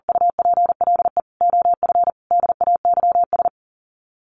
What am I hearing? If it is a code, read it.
Morse code, heard as a keyed tone, about 40 wpm, 700 Hz. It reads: UPLE OF DAYS
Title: Sets of 5 Words Encore2 40wpm